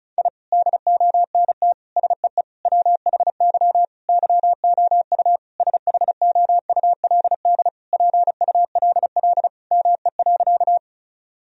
Morse code, heard as a keyed tone, about 36 wpm, 700 Hz. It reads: I DONT SEE WHY YOU SHOULD PULL ME.
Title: Sentences from Top 1000 Words - Review Part 2 - 35wpm